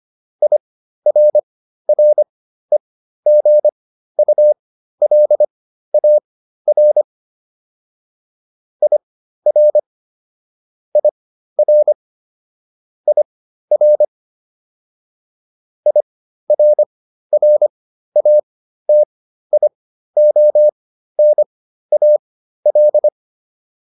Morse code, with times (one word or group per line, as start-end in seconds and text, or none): 0.4-7.0 IRREGULAR
8.8-9.8 IR
11.0-11.9 IR
13.1-14.1 IR
15.9-23.1 IRRATIONAL